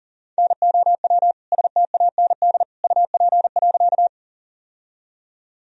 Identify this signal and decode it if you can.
Morse code, heard as a keyed tone, about 40 wpm, 700 Hz. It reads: NOW STAND UP.
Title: Sentences from Top 1000 Words - Review Part 1 - 40wpm